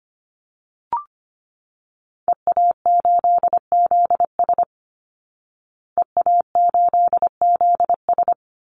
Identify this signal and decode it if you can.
Morse code, heard as a keyed tone, about 25 wpm, 700 Hz. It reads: EA8ZS EA8ZS
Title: Call Signs Repeated 2x - 25wpm